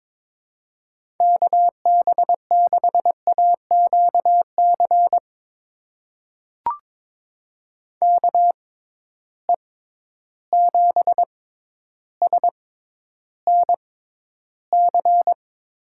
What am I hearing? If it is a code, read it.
Morse code, heard as a keyed tone, about 22 wpm, spaced as for 11 wpm, 700 Hz. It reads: KB6AQC KE7SNC
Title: USA Call Signs - 6 Characters - 6x Character Spacing - 22wpm